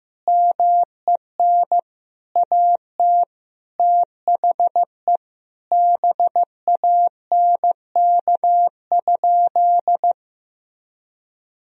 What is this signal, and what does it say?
Morse code, heard as a keyed tone, about 16 wpm, 700 Hz. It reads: MEN AT THE BANK?
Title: Sentences from Top 1000 Words - Review Part 4 - 15wpm